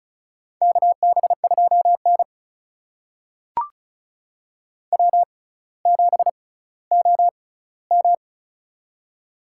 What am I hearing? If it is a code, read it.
Morse code, heard as a keyed tone, about 36 wpm, spaced as for 18 wpm, 700 Hz. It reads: KB2N W7OM
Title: USA Call Signs - 4 Characters - 6x Character Spacing - 35wpm